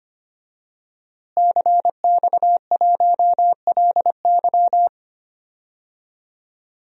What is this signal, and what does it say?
Morse code, heard as a keyed tone, about 25 wpm, 700 Hz. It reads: CX1LY
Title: DXCC-100 Call Signs - 5 Characters - Repeated 2x - 25wpm